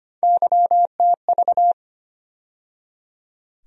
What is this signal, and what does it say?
Morse code, heard as a keyed tone, about 25 wpm, 700 Hz. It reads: YTV
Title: USA Call Sign Postfixes - 3 Letters 25wpm